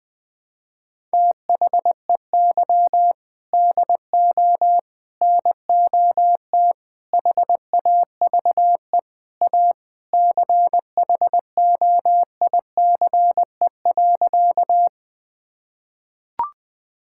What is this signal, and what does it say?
Morse code, heard as a keyed tone, about 20 wpm, 700 Hz. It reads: THEY DO NOT HAVE A CHOICE.
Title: Sentences with 4-Word Phrases - 20wpm